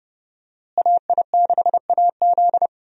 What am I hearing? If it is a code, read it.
Morse code, heard as a keyed tone, about 30 wpm, 700 Hz. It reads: AI6AZ
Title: USA Call Signs - 5 Characters - Repeated 2x - 30wpm